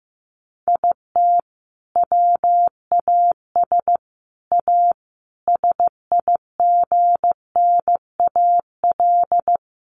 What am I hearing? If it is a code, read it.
Morse code, heard as a keyed tone, about 15 wpm, 700 Hz. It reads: IT WAS A SIGNAL
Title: Sentences from Top 1400 Words 15wpm